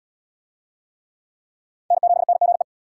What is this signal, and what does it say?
Morse code, heard as a keyed tone, about 68 wpm, 700 Hz. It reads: N7NGE